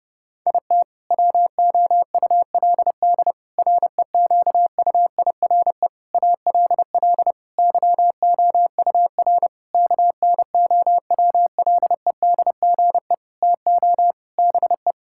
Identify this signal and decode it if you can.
Morse code, heard as a keyed tone, about 30 wpm, 700 Hz. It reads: IT WOULD REQUIRE ALL YOUR KNOWLEDGE TO BE
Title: Sentences from Top 1000 Words - Review Part 6 - 30wpm